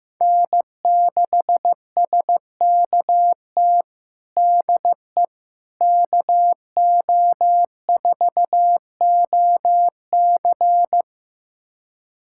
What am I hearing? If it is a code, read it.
Morse code, heard as a keyed tone, about 15 wpm, 700 Hz. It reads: N6SKT DE KO4OC
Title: USA Callsign DE Callsign - 5 Characters - 15wpm